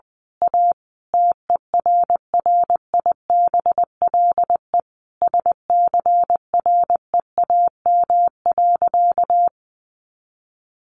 Morse code, heard as a keyed tone, about 20 wpm, 700 Hz. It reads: A TERRIBLE SCREAM.
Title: A Sound Of Thunder 20wpm